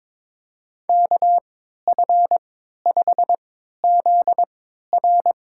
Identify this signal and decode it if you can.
Morse code, heard as a keyed tone, about 22 wpm, spaced as for 12 wpm, 700 Hz. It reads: KF5ZR
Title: USA Call Signs - 3x Character Spacing - 22wpm